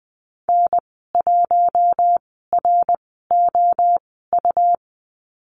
Morse code, heard as a keyed tone, about 20 wpm, spaced as for 14 wpm, 700 Hz. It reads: N1ROU
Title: 140 Lesson 23g - Letter f - Call Signs 20wpm